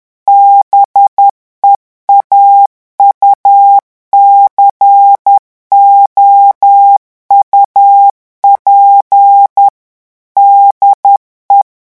Morse code, heard as a keyed tone, about 10 wpm, 800 Hz. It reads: BEAUCOUP DE
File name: revision9 citations humoristiques vitesse 10 mots minute